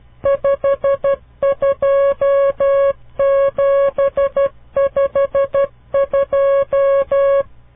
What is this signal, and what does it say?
Morse code, heard as a keyed tone, about 12 wpm, 550 Hz. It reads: 52752
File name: CW Learning Programs / farnsworth / unit 5